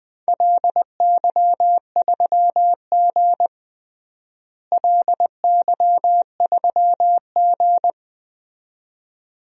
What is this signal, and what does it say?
Morse code, heard as a keyed tone, about 20 wpm, 700 Hz. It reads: LY3G LY3G
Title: DXCC-100 Call Signs - 4 Characters - Repeated 2x - 20wpm